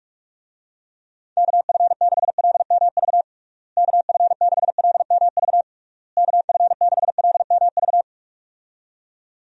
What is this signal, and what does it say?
Morse code, heard as a keyed tone, about 46 wpm, 700 Hz. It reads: KF6LMV KF6LMV KF6LMV
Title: USA Call Signs - 6 Characters - Repeated 3x - 45wpm